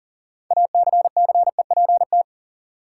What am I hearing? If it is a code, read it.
Morse code, heard as a keyed tone, about 38 wpm, 700 Hz. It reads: ACCEPT
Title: Sentences from Top 1300 Words - New Word Drill - 40wpm